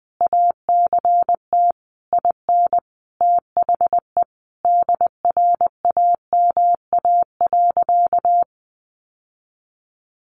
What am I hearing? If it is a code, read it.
Morse code, heard as a keyed tone, about 20 wpm, 700 Hz. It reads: ACT IN THE DRAMA.
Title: Sentences from Top 1500 Words 20wpm